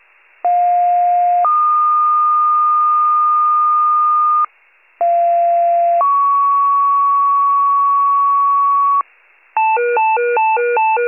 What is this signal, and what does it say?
Two-tone page, heard as a one-second tone, then a three-second tone, about 710 then 1220 Hz, about 690 then 1120 Hz. It is heard from 0.4 to 9.0 s.